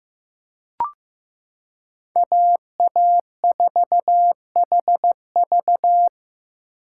Morse code, heard as a keyed tone, about 15 wpm, 700 Hz. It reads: AA4HV
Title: K1USN SST QSO Practice - 15wpm